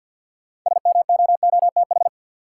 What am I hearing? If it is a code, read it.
Morse code, heard as a keyed tone, about 46 wpm, 700 Hz. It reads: SMOOTH
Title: Sentences from Top 1400 Words - New Word Drill Alt - 50wpm